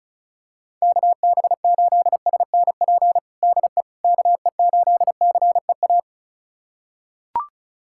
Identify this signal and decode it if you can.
Morse code, heard as a keyed tone, about 36 wpm, 700 Hz. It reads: KB8SNP DE KE8CEA
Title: USA Callsign DE Callsign - 6 Characters - 35wpm